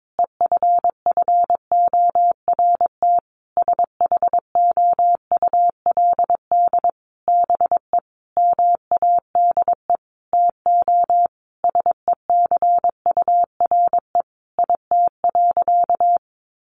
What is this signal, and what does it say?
Morse code, heard as a keyed tone, about 22 wpm, 700 Hz. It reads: EFFORT SHOULD BE MADE TO SECURE IT.
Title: Sentences from Top 1200 Words 22wpm